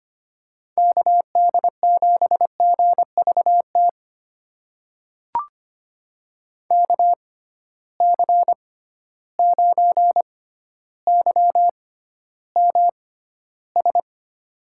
Morse code, heard as a keyed tone, about 25 wpm, spaced as for 13 wpm, 700 Hz. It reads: KD7GVT KC9YMS
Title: USA Call Signs - 6x Character Spacing - 25wpm